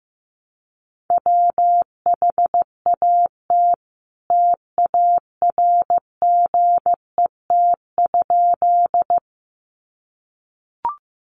Morse code, heard as a keyed tone, about 15 wpm, 700 Hz. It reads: WHAT TARGET?
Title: Sentences from Top 1200 Words 15wpm